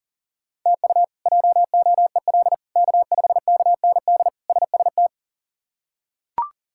Morse code, heard as a keyed tone, about 37 wpm, 690 Hz. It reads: TU JOEL K5KND SST
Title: K1USN SST QSO Practice - 40wpm